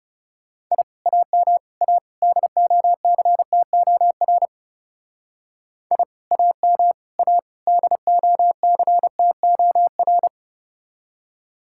Morse code, heard as a keyed tone, about 34 wpm, 700 Hz. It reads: I AM A DOCTOR I AM A DOCTOR
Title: Speed-Racing Common QSO Phrases - 30wpm